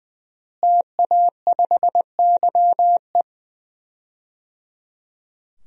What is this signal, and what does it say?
Morse code, heard as a keyed tone, about 20 wpm, 700 Hz. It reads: TA5YE
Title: DXCC-100 Call Signs - 5 Characters - Repeated 3x - 20wpm